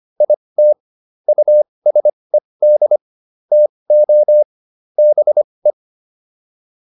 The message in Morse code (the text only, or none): IT USED TO BE